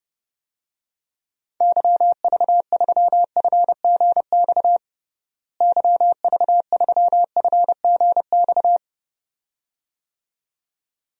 Morse code, heard as a keyed tone, about 30 wpm, 700 Hz. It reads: YV3FGX YV3FGX
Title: DXCC-100 Call Signs - 6 Characters - Repeated 2x - 30wpm